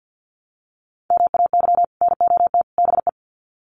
Morse code, heard as a keyed tone, about 50 wpm, 700 Hz. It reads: MAY NOT BE
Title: Sets of 3 Words 50wpm